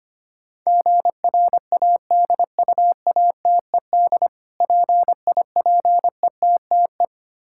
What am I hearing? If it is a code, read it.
Morse code, heard as a keyed tone, about 25 wpm, 700 Hz. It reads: GRADUATED PIPETTE
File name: practice / 25wpm / Difficult Words - Speed-Racing with Hints - 25wpm